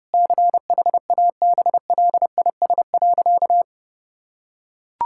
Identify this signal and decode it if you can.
Morse code, heard as a keyed tone, about 30 wpm, 700 Hz. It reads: CHABLIS.
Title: Pangram Challenge v2 - 30wpm